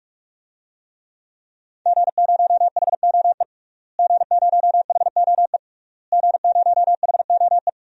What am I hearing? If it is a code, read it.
Morse code, heard as a keyed tone, about 46 wpm, 700 Hz. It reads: G0HOE G0HOE G0HOE
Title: DXCC-100 Call Signs - 5 Characters - Repeated 3x - 45wpm